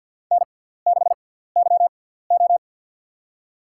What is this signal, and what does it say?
Morse code, heard as a keyed tone, about 52 wpm, spaced as for 17 wpm, 700 Hz. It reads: N6YO
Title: USA Call Signs - 4 Characters - 6x Character Spacing - 50wpm